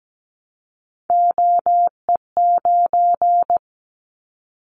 Morse code, heard as a keyed tone, about 17 wpm, 700 Hz. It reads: OE9
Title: Callsign Prefixes for Top 25 DXCC - 17wpm